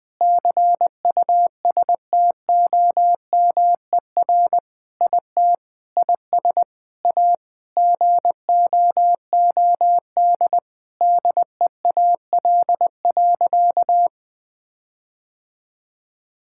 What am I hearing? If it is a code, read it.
Morse code, heard as a keyed tone, about 20 wpm, 700 Hz. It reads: CUSTOMER IT IS A GOOD DEAL.
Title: Sentences with 7-Word Phrases - Spoken Beginning - 20wpm